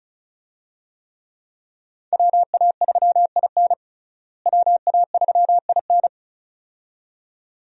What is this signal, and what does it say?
Morse code, heard as a keyed tone, about 36 wpm, 700 Hz. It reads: WA3IN WA3IN
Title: USA Call Signs - 5 Characters - Repeated 2x - 35wpm